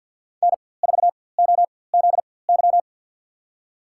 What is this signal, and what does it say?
Morse code, heard as a keyed tone, about 51 wpm, spaced as for 23 wpm, 700 Hz. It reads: N4OZY
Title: USA Call Signs - 5 Characters - 4x Character Spacing - 50wpm